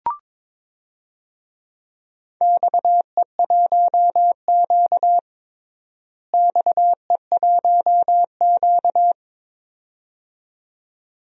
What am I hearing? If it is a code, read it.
Morse code, heard as a keyed tone, about 22 wpm, 700 Hz. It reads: XE1Q XE1Q
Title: DXCC-100 Call Signs - 4 Characters - Repeated 2x - 22wpm